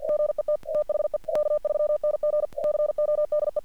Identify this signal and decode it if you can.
Morse code, heard as a keyed tone, about 47 wpm, 600 Hz. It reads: GET THE K3NG CODE
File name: archived / CRSnet